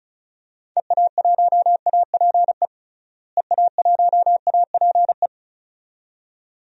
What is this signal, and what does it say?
Morse code, heard as a keyed tone, about 34 wpm, 700 Hz. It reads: EA1APE EA1APE
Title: DXCC-100 Call Signs - 6 Characters - Repeated 2x - 35wpm